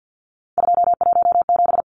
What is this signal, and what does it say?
Morse code, heard as a keyed tone, about 50 wpm, 700 Hz. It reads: 217
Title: Numbers - 3-Digit 50wpm